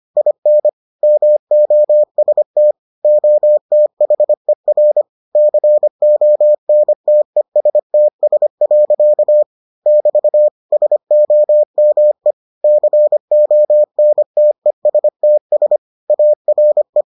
Morse code, heard as a keyed tone, about 25 wpm, 600 Hz. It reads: IN MOST OTHER CONTESTS. <BT> SOME CONTESTS ARE